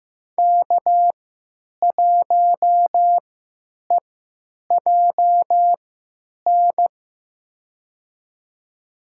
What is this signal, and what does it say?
Morse code, heard as a keyed tone, about 15 wpm, spaced as for 9 wpm, 700 Hz. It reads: K1EJN
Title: USA Call Signs - 3x Character Spacing - 15wpm